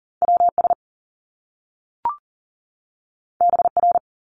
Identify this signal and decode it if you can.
Morse code, heard as a keyed tone, about 39 wpm, 700 Hz. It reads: WS BR
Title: Top Two Letter Combinations in Words 40wpm